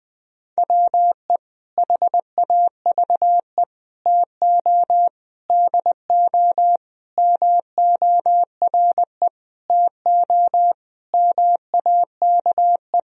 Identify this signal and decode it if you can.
Morse code, heard as a keyed tone, about 20 wpm, 700 Hz. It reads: WE HAVE TO DO MORE TO MAKE
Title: Sentences from Top 1000 Words - Review Part 3 - 20wpm